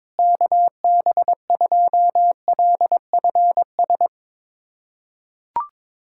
Morse code, heard as a keyed tone, about 21 wpm, 700 Hz. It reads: KB2LFS
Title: USA Callsign DE Callsign - 6 Characters - 22wpm